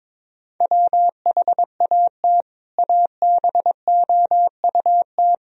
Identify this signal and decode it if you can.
Morse code, heard as a keyed tone, about 22 wpm, 700 Hz. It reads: WHAT ABOUT
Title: Sentences from Top 1000 Words - Review Part 1 - 22wpm